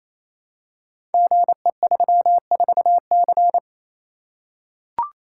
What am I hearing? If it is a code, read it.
Morse code, heard as a keyed tone, about 28 wpm, 690 Hz. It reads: GE34C
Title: Code Groups - 5 Letter:Numbers - 28wpm